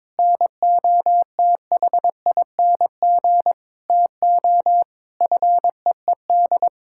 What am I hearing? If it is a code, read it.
Morse code, heard as a keyed tone, about 21 wpm, 700 Hz. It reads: NOTHING TO FEED
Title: Sentences from Top 1600 Words 22wpm